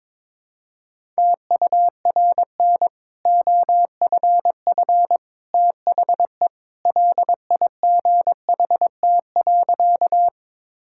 Morse code, heard as a keyed tone, about 21 wpm, 700 Hz. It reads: TURN OFF THE LIGHT.
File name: practice / 22wpm / Easy Sentences with Hints - 22wpm